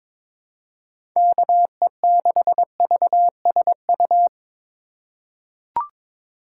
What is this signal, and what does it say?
Morse code, heard as a keyed tone, about 22 wpm, 700 Hz. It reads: KE6VSU
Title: USA Call Signs - 6 Characters - 2x Character Spacing - 22wpm